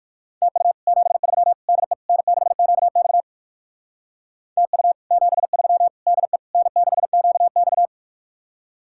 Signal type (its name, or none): Morse code